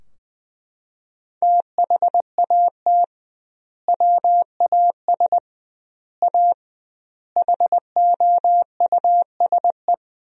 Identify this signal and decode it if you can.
Morse code, heard as a keyed tone, about 20 wpm, 700 Hz. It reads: THAT WAS A HOUSE
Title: 2x Word-Spacing - Sentences from Top 200 Words - 20wpm